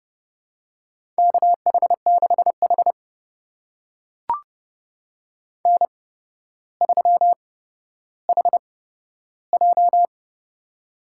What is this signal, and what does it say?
Morse code, heard as a keyed tone, about 30 wpm, spaced as for 13 wpm, 700 Hz. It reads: KH6H N3HJ
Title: USA Call Signs - 4 Characters - 8x Character Spacing - 30wpm